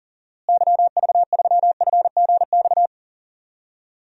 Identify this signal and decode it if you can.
Morse code, heard as a keyed tone, about 40 wpm, 700 Hz. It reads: YV3FGX
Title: DXCC-100 Call Signs - 6 Characters - Repeated 2x - 40wpm